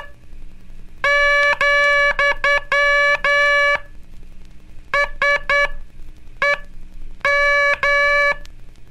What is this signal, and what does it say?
Morse code, heard as a keyed tone, about 9 wpm, spaced as for 7 wpm, 1700 Hz, with noise. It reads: , SEM
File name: cursos / Curso CW mp3 / CW Aula - 15